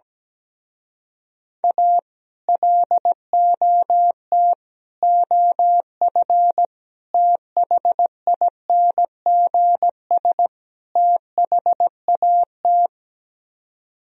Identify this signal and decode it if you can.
Morse code, heard as a keyed tone, about 17 wpm, 700 Hz. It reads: A LOT OF THINGS THAT
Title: Speed-Racing - Sets of 5 Words 17wpm